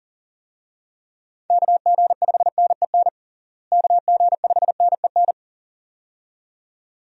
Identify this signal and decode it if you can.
Morse code, heard as a keyed tone, about 37 wpm, 700 Hz. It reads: KG5NEN KG5NEN
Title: USA Call Signs - 6 Characters - Repeated 2x - 40wpm